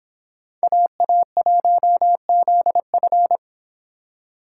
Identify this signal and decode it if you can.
Morse code, heard as a keyed tone, about 29 wpm, 700 Hz. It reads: AA1ZF